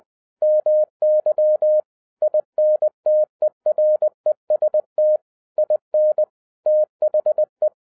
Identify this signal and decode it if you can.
Morse code, heard as a keyed tone, about 20 wpm, 610 Hz. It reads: MY INTEREST IN THE
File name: 2012-01-mp3 / n05-CAME BACK TO HAM RADIO AFTER 30 YEARS OF INTERVAL-20wpm